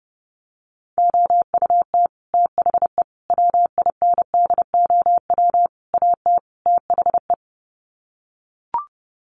Morse code, heard as a keyed tone, about 30 wpm, 700 Hz. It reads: OUT THE WINDOW AT THE
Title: Sets of 5 Words Encore2 30wpm